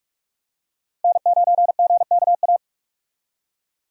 Morse code, heard as a keyed tone, about 42 wpm, 700 Hz. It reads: N9GKA
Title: USA Call Signs - 5 Characters - 45wpm